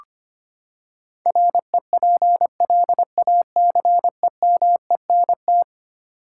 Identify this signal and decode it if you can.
Morse code, heard as a keyed tone, about 25 wpm, 700 Hz. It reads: REPLACEMENT
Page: morse-code-ninja-practice-25wpm